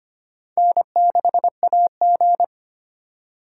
Morse code, heard as a keyed tone, about 25 wpm, 700 Hz. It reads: N6AG
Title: USA Call Signs - 4 Characters - 2x Character Spacing - 25wpm